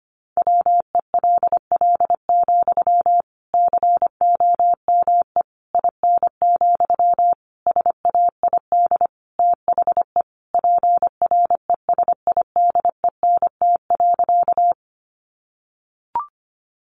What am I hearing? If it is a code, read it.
Morse code, heard as a keyed tone, about 25 wpm, 700 Hz. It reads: WELL, COME IN, SAID THE PRESIDENT.
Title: Sentences from Top 1000 Words 25wpm